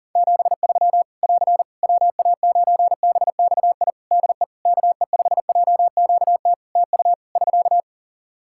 Morse code, heard as a keyed tone, about 40 wpm, 700 Hz. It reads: 73 <AR> WA9BXI DE KE5JQT TU <SK>
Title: Practice QSO - Exchange 4 - Ending - 40wpm